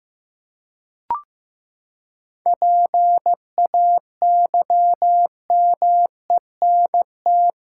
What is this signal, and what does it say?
Morse code, heard as a keyed tone, about 15 wpm, 700 Hz. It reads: PAYMENT
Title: Seven Letter Words - 15wpm